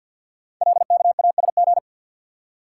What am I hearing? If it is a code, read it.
Morse code, heard as a keyed tone, about 48 wpm, 690 Hz. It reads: LKASG
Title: Code Groups - 5 Letters - 50wpm